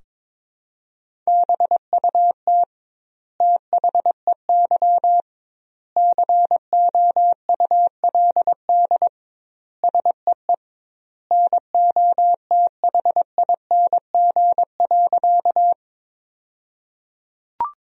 Morse code, heard as a keyed tone, about 22 wpm, 700 Hz. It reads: BUT THEY COULD SEE NOTHING.
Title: 2x Word-Spacing - Sentences from Top 500 Words - 22wpm